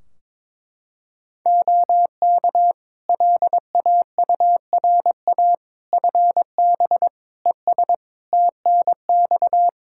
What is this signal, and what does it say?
Morse code, heard as a keyed tone, about 22 wpm, 700 Hz. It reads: OK LAURA FB ES TNX
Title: K1USN SST QSO Practice 22wpm